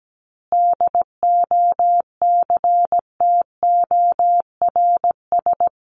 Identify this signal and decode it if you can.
Morse code, heard as a keyed tone, about 17 wpm, 700 Hz. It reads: DOCTORS
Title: Easy Phrases 17wpm